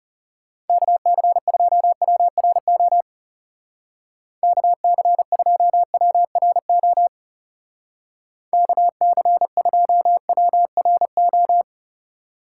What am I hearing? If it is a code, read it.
Morse code, heard as a keyed tone, about 33 wpm, 700 Hz. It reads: KC2WRO KC2WRO KC2WRO
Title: Speed Racing - USA 2x3 Call Signs 30wpm